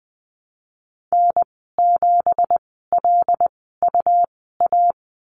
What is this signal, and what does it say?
Morse code, heard as a keyed tone, about 20 wpm, spaced as for 15 wpm, 700 Hz. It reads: N7LUA
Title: 193 Lesson 31g - Letter v - Call Signs 20wpm